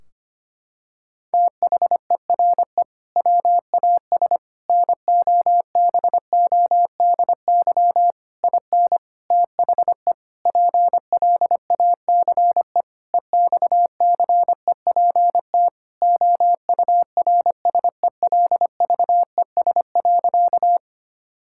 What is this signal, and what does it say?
Morse code, heard as a keyed tone, about 25 wpm, 700 Hz. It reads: THERE WAS NOBODY IN THE PLACE EXCEPT OURSELVES.
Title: Sentences from Top 1600 Words 25wpm